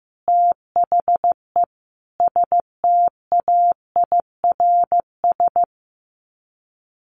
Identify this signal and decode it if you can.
Morse code, heard as a keyed tone, about 15 wpm, 700 Hz. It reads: THE STAIRS
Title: Sets of 5 Words Encore 15wpm